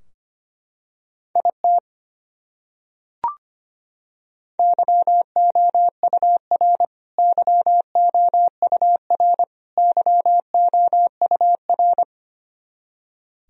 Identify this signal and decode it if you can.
Morse code, heard as a keyed tone, about 25 wpm, 700 Hz. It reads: IT YOUR YOUR YOUR
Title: Top 100 Words Repeated 25wpm